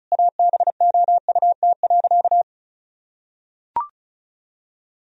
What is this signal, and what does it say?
Morse code, heard as a keyed tone, about 36 wpm, 700 Hz. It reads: ABOUT.
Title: Sentences from Top 1000 Words - Review Part 2 - 35wpm